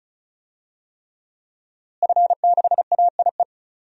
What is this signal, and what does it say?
Morse code, heard as a keyed tone, about 36 wpm, 700 Hz. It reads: F6AIE
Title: DXCC-100 Call Signs - 5 Characters - Repeated 3x - 35wpm